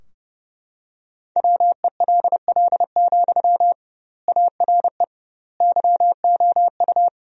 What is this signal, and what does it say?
Morse code, heard as a keyed tone, about 30 wpm, 700 Hz. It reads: WELL, ARE YOU
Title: 2x Word-Spacing - Sentences from Top 400 Words - 30wpm